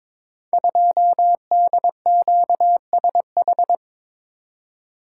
Morse code, heard as a keyed tone, about 22 wpm, 700 Hz. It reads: 2DQSH
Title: Code Groups - 5 Letter:Numbers - 22wpm